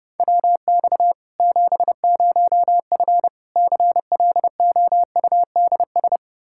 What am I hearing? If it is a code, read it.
Morse code, heard as a keyed tone, about 30 wpm, 700 Hz. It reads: WX 70F CLOUDS
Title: K1USN SST QSO Practice 30wpm